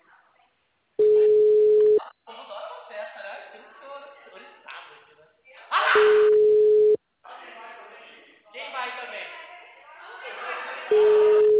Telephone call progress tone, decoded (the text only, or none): ringback tone